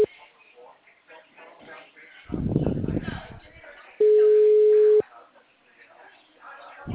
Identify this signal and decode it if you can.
Ringback tone (Europe, 425 Hz), with speech.